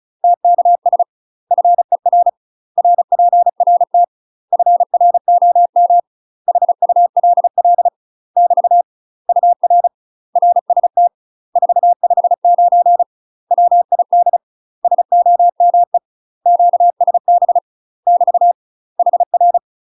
Morse code, heard as a keyed tone, about 36 wpm, 700 Hz. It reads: TKS FER RPRT FROM HULL <BT> UR RST 459 WID SOME QSB <BT> HR
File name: UK MASTER QSOs TRAINING / 35WPM / UK MASTER QSOs 0023